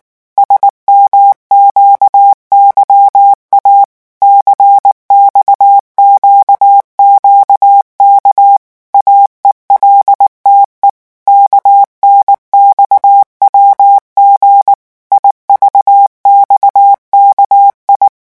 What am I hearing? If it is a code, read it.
Morse code, heard as a keyed tone, about 19 wpm, 800 Hz. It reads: SMQYA CXQQK AELTE KNXWG IVXKI
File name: morse mp3 / revision7-vitesse 18 mots